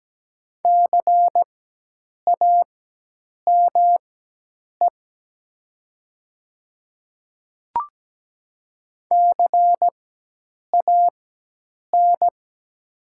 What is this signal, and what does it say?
Morse code, heard as a keyed tone, about 17 wpm, spaced as for 8 wpm, 700 Hz. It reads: CAME CAN
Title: 4x Character Spacing - Top 500 Words - 17wpm